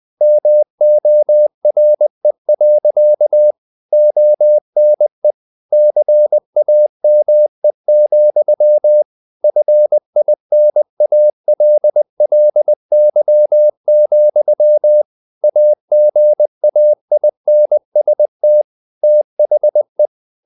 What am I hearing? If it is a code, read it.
Morse code, heard as a keyed tone, about 20 wpm, 600 Hz. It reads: MORE. ONE CAME, FINALLY, AGAINST THE
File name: the old man and the sea part 2 20wpm